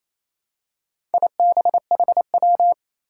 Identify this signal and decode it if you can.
Morse code, heard as a keyed tone, about 27 wpm, 700 Hz. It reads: IBHW
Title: Four Challenging Letter-Number Combinations 28wpm